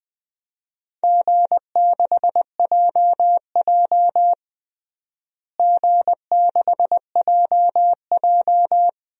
Morse code, heard as a keyed tone, about 20 wpm, 700 Hz. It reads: G6JJ G6JJ
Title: DXCC-100 Call Signs - 4 Characters - Repeated 2x - 20wpm